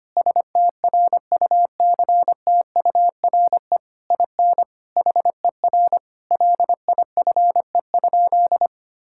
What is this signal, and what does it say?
Morse code, heard as a keyed tone, about 25 wpm, 700 Hz. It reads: STRUCTURE IN HER LIFE?
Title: Sentences from Top 1000 Words - Review Part 6 - 25wpm